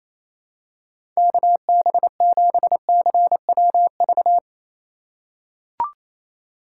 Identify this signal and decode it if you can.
Morse code, heard as a keyed tone, about 27 wpm, 700 Hz. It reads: KB7CWV
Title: USA Call Signs - 6 Characters - 3x Character Spacing - 28wpm